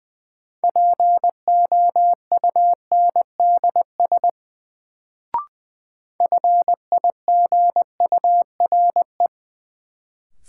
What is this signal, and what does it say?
Morse code, heard as a keyed tone, about 20 wpm, 700 Hz. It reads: POUNDS FIGURE
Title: Sentences from Top 1000 Words - Review Part 2 - New Word Drill - 20wpm